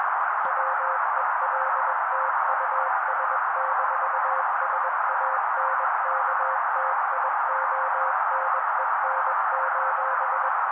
Morse code, heard as a keyed tone, about 21 wpm, 530 Hz. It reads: WELT US-SANKTIONEN: